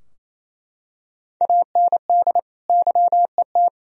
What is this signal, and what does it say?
Morse code, heard as a keyed tone, about 28 wpm, 700 Hz. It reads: AND YET
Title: Sentences from Top 1000 Words - Review Part 3 - 28wpm